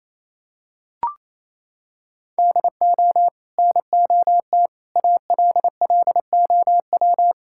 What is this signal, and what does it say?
Morse code, heard as a keyed tone, about 27 wpm, 700 Hz. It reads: DO NOT ALLOW
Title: Sentences from Top 1000 Words - Review Part 5 - 28wpm